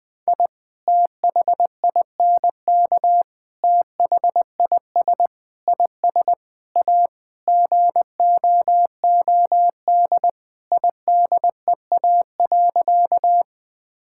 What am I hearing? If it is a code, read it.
Morse code, heard as a keyed tone, about 20 wpm, 700 Hz. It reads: I THINK THIS IS A GOOD IDEA.
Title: Sentences with 5-Word Phrases - 20wpm